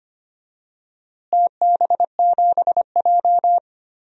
Morse code, heard as a keyed tone, about 25 wpm, 700 Hz. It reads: TB7J
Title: DXCC-100 Call Signs - 4 Characters - Repeated 2x - 25wpm